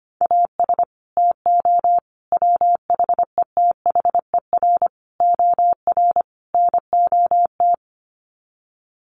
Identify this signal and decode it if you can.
Morse code, heard as a keyed tone, about 25 wpm, 700 Hz. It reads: AS TO WHETHER OR NOT
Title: Sets of 5 Words Encore 25wpm